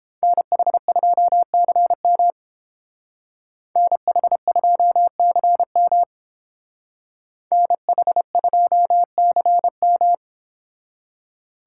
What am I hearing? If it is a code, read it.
Morse code, heard as a keyed tone, about 30 wpm, 700 Hz. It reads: NH2CM NH2CM NH2CM